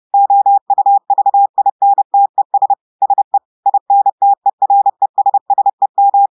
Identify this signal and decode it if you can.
Morse code, heard as a keyed tone, about 30 wpm, 800 Hz. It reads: OUVINTES SE INTERESSEM
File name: audios / portugais / portugais texte en clair 2 vitesse 30 mots minutes